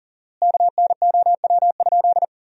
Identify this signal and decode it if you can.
Morse code, heard as a keyed tone, about 40 wpm, 700 Hz. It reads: KNOW?
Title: 4x Word-Spacing - Sentences from Top 100 Words - 40wpm